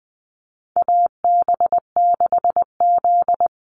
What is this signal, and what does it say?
Morse code, heard as a keyed tone, about 20 wpm, 700 Hz. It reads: AB6Z
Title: Call Signs Repeated 3x - 20wpm